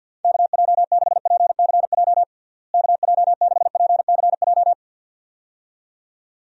Morse code, heard as a keyed tone, about 52 wpm, 700 Hz. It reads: KJ6PCJ KJ6PCJ
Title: USA Call Signs - 6 Characters - Repeated 2x - 50wpm